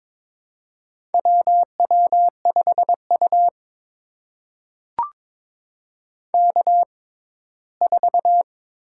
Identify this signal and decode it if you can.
Morse code, heard as a keyed tone, about 23 wpm, spaced as for 15 wpm, 700 Hz. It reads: WW5U K4
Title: USA Call Signs - 4 Characters - 6x Character Spacing - 22wpm